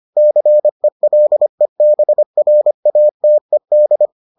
Morse code, heard as a keyed tone, about 25 wpm, 600 Hz. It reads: CELEBRATED